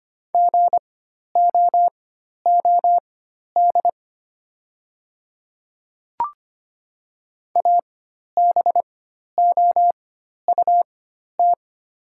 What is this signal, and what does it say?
Morse code, heard as a keyed tone, about 25 wpm, spaced as for 12 wpm, 700 Hz. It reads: GOOD ABOUT
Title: 4x Character Spacing - Top 200 Words - 25wpm